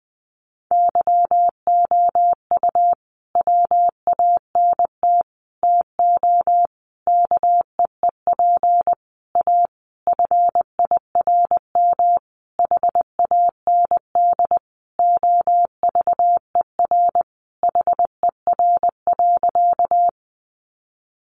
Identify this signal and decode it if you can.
Morse code, heard as a keyed tone, about 20 wpm, 700 Hz. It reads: YOU WANT TO KEEP A FIRM HAND OVER HER.
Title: Sentences from Top 1400 Words 20wpm